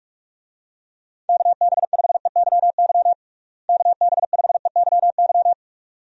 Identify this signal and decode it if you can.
Morse code, heard as a keyed tone, about 46 wpm, 700 Hz. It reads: KB5EYY KB5EYY
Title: USA Call Signs - 6 Characters - Repeated 2x - 45wpm